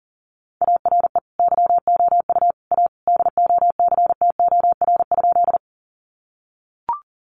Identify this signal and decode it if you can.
Morse code, heard as a keyed tone, about 40 wpm, 700 Hz. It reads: ARE YOU A DOCTOR?
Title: Sentences from Top 1500 Words 40wpm